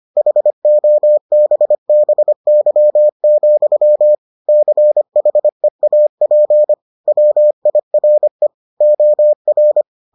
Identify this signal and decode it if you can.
Morse code, heard as a keyed tone, about 25 wpm, 600 Hz. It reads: HOBBY, CHEAP WIRE OR